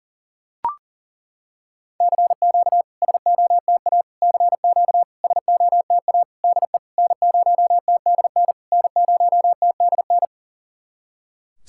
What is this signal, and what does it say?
Morse code, heard as a keyed tone, about 40 wpm, 700 Hz. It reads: CQ SOTA CQ SOTA DE N0TDN N0TDN
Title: SOTA QSO Practice 40wpm